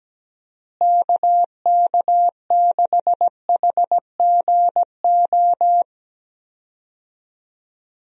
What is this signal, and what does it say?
Morse code, heard as a keyed tone, about 17 wpm, 700 Hz. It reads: KK6HGO
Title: USA Call Signs - 6 Characters - Repeated 2x - 17wpm